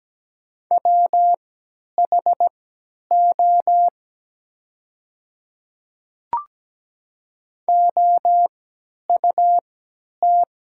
Morse code, heard as a keyed tone, about 17 wpm, spaced as for 10 wpm, 700 Hz. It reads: WHO OUT
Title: 3x Character Spacing - Top 300 Words - 17wpm